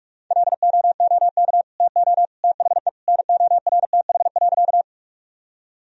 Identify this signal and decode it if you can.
Morse code, heard as a keyed tone, about 46 wpm, 700 Hz. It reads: LOOK TO THE NORTH.
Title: Sentences from Top 1000 Words - Review Part 1 - 45wpm